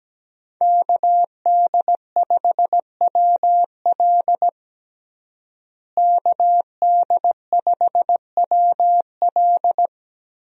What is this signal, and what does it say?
Morse code, heard as a keyed tone, about 17 wpm, 700 Hz. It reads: KD5WL KD5WL
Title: USA Call Signs - 5 Characters - Repeated 2x - 17wpm